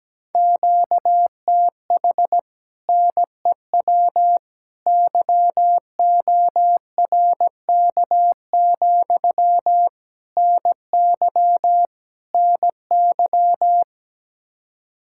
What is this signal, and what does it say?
Morse code, heard as a keyed tone, about 17 wpm, 700 Hz. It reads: QTH NEW YORK, NY NY
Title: Speed-Racing Common QSO Phrases - 17wpm